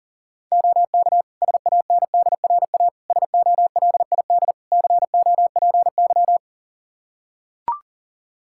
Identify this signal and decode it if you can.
Morse code, heard as a keyed tone, about 37 wpm, 700 Hz. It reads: OK SANDRA SOLID COPY
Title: Practice QSO - Exchange 3 - More Info - 40wpm